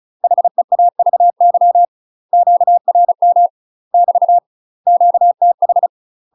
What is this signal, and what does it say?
Morse code, heard as a keyed tone, about 35 wpm, 700 Hz. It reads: HEAVY QRM <BT> QTH
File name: UK MASTER QSOs TRAINING/35WPM